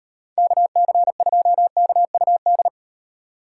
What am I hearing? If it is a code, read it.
Morse code, heard as a keyed tone, about 38 wpm, 700 Hz. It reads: KC2KUD